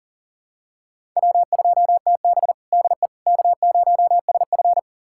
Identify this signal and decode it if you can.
Morse code, heard as a keyed tone, about 40 wpm, 700 Hz. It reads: W2TB DE K0SF
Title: USA Callsign DE Callsign - 4 Characters - 40wpm